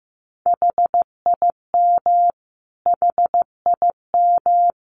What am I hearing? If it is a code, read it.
Morse code, heard as a keyed tone, about 15 wpm, 700 Hz. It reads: HIM HIM
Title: Top 200 Words Repeated 15wpm